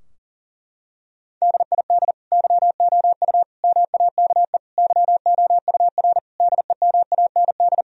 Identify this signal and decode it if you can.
Morse code, heard as a keyed tone, about 40 wpm, 700 Hz. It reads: DID YOU MAKE YOUR DEMAND
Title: Sentences from Top 1000 Words - Review Part 4 - 40wpm